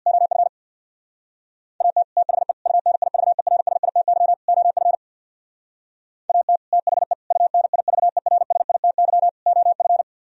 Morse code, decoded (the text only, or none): OF AT THE UNIVERSITY OF AT THE UNIVERSITY OF